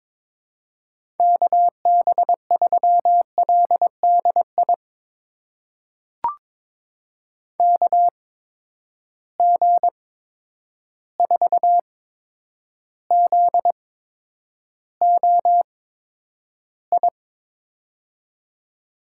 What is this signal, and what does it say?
Morse code, heard as a keyed tone, about 22 wpm, spaced as for 9 wpm, 700 Hz. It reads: KB3LDI KG4ZOI
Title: USA Call Signs - 6 Characters - 8x Character Spacing - 22wpm